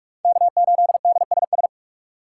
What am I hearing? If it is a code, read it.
Morse code, heard as a keyed tone, about 46 wpm, 700 Hz. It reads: K8DSS
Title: K1USN SST QSO Practice - 45wpm